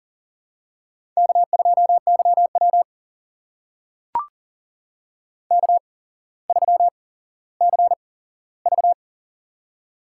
Morse code, heard as a keyed tone, about 41 wpm, spaced as for 17 wpm, 700 Hz. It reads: K2YW K3CV
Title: USA Call Signs - 4 Characters - 8x Character Spacing - 40wpm